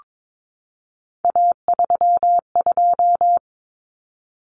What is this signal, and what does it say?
Morse code, heard as a keyed tone, about 22 wpm, 700 Hz. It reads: A32
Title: Common Cut Numbers - 3-Digit 22wpm